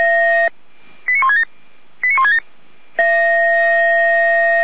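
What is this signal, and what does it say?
Five-tone selective call (ZVEI), heard as quick runs of five tones, tones of about 80 ms. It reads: 89167 89167
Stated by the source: ZVEI five-tone selective call received on SDR.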